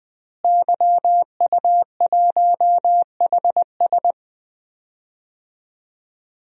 Morse code, heard as a keyed tone, about 20 wpm, 700 Hz. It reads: YU1HS
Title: DXCC-100 Call Signs - 5 Characters - Repeated 2x - 20wpm